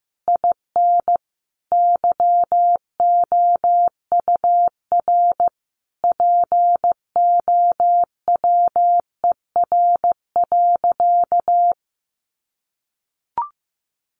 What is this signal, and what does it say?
Morse code, heard as a keyed tone, about 15 wpm, 700 Hz. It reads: IN YOUR POWER.
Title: Sentences from Top 600 Words 15wpm